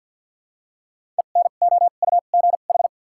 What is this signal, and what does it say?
Morse code, heard as a keyed tone, about 53 wpm, spaced as for 37 wpm, 700 Hz. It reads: ENOUGH
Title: 2x Character Spacing - Top 500 Words - 50wpm